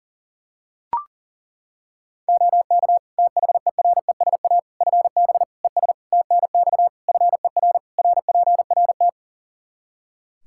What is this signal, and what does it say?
Morse code, heard as a keyed tone, about 40 wpm, 700 Hz. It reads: OK THERESA FB ES TNX FER RPRT
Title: Practice QSO - Exchange 2 - Info - 40wpm